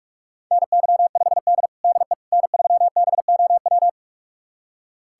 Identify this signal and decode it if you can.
Morse code, heard as a keyed tone, about 41 wpm, 690 Hz. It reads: NY5D DE N3BOW
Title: USA Callsign DE Callsign 45wpm